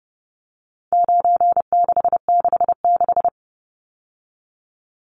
Morse code, heard as a keyed tone, about 30 wpm, 700 Hz. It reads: 9666
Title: Numbers - 4-Digit 30wpm